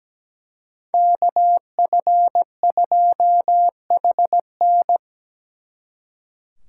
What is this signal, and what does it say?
Morse code, heard as a keyed tone, about 20 wpm, 700 Hz. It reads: KF2HN
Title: Speed Racing - USA 2x2 Call Signs 17wpm